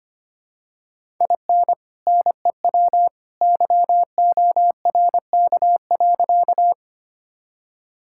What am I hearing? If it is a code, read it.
Morse code, heard as a keyed tone, about 25 wpm, 700 Hz. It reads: IN NEW YORK.
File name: practice / 25wpm / Sentences with 2-Word Phrases - 25wpm